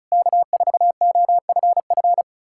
Morse code, heard as a keyed tone, about 40 wpm, 700 Hz. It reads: K4OFF